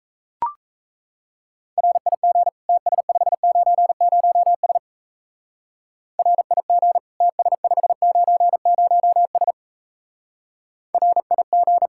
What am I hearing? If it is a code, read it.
Morse code, heard as a keyed tone, about 34 wpm, 700 Hz. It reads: RIG TS590S RIG TS590S RIG
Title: Speed-Racing Common QSO Phrases - 28wpm